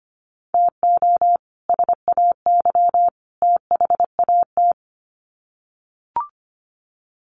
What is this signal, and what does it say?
Morse code, heard as a keyed tone, about 25 wpm, 700 Hz. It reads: TO SAY THAT
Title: Sets of 3 Words 25wpm